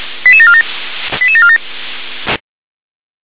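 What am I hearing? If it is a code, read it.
Five-tone selective call (ZVEI), heard as quick runs of five tones, tones of about 70 ms. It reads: 88747 88747
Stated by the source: ZVEI five-tone selective call received on an SDR.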